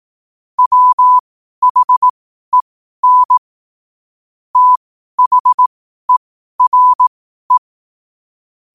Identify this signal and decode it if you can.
Morse code, heard as a keyed tone, about 18 wpm, spaced as for 12 wpm, 1000 Hz. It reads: WHEN THERE
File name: qotdfiles / 12wpm